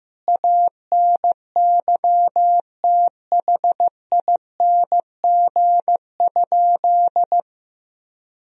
Morse code, heard as a keyed tone, about 15 wpm, 700 Hz. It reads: ANYTHING?
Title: Sentences from Top 1000 Words - Review Part 2 - 15wpm